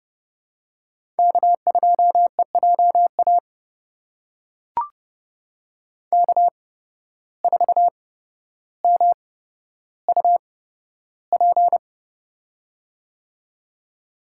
Morse code, heard as a keyed tone, about 32 wpm, spaced as for 13 wpm, 700 Hz. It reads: K2EJA K4MUP
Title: USA Call Signs - 5 Characters - 8x Character Spacing - 30wpm